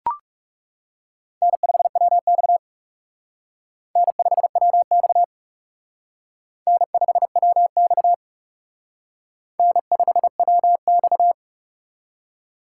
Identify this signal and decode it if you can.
Morse code, heard as a keyed tone, about 37 wpm, 700 Hz. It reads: N5WX N5WX N5WX N5WX
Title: Speed Racing - USA Call Signs 30wpm